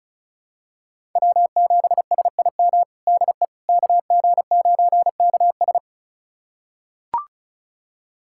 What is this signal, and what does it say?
Morse code, heard as a keyed tone, about 34 wpm, 700 Hz. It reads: W7SIM DE KG9KS
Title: USA Callsign DE Callsign - 5 Characters - 35wpm